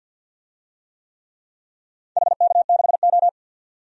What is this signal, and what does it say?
Morse code, heard as a keyed tone, about 47 wpm, 700 Hz. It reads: HK6O
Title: DXCC-100 Call Signs - 4 Characters - 50wpm